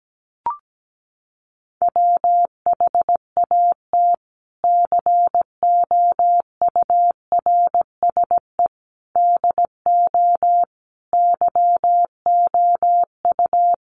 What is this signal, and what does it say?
Morse code, heard as a keyed tone, about 17 wpm, 700 Hz. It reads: WHAT COURSE DO YOU
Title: Sentences from Top 1200 Words 17wpm